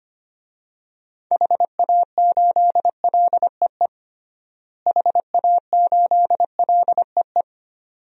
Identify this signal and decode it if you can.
Morse code, heard as a keyed tone, about 25 wpm, 700 Hz. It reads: HA8LEE HA8LEE
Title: DXCC-100 Call Signs - 6 Characters - Repeated 2x - 25wpm